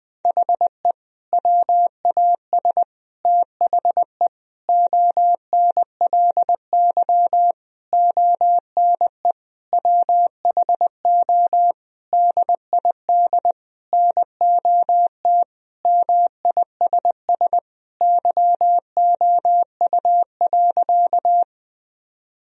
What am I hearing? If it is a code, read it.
Morse code, heard as a keyed tone, about 19 wpm, 700 Hz. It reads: HE WAS THE ONLY ONE WHO DID NOT MISS YOU.
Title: Sentences from Top 1000 Words - Review Part 3 - 20wpm